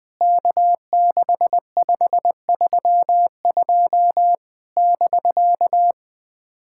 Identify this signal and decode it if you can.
Morse code, heard as a keyed tone, about 20 wpm, 700 Hz. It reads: K6532 <BK>
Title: POTA QSO Practice - 20wpm